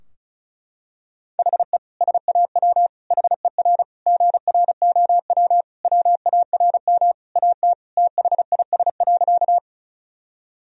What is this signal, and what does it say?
Morse code, heard as a keyed tone, about 36 wpm, 700 Hz. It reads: HE SAW HER GROW WARM AT THIS.
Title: Sentences from Top 1000 Words - Review Part 2 - 35wpm